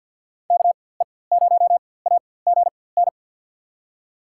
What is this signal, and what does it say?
Morse code, heard as a keyed tone, about 51 wpm, spaced as for 23 wpm, 700 Hz. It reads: KE0AGN
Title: USA Call Signs - 6 Characters - 4x Character Spacing - 50wpm